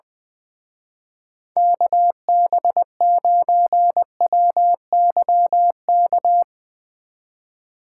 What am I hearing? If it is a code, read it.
Morse code, heard as a keyed tone, about 21 wpm, 700 Hz. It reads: KB9WYK
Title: Speed Racing - USA 2x3 Call Signs 20wpm